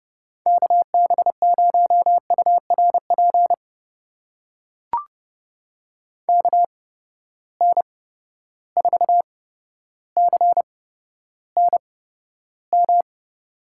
Morse code, heard as a keyed tone, about 30 wpm, spaced as for 13 wpm, 700 Hz. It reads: KB0URP KN4CNM
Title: USA Call Signs - 6 Characters - 8x Character Spacing - 30wpm